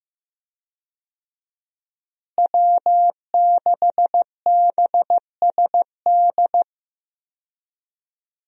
Morse code, heard as a keyed tone, about 15 wpm, 700 Hz. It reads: W6BSD
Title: USA Call Signs - 5 Characters - Repeated 3x - 15wpm